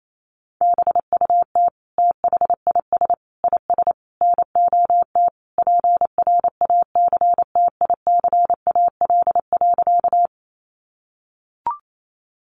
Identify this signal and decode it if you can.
Morse code, heard as a keyed tone, about 28 wpm, 700 Hz. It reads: BUT THIS IS NOT PRACTICAL.
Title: Sentences from Top 1400 Words 28wpm